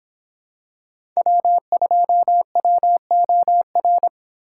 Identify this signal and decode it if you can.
Morse code, heard as a keyed tone, about 26 wpm, 700 Hz. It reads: W2WOR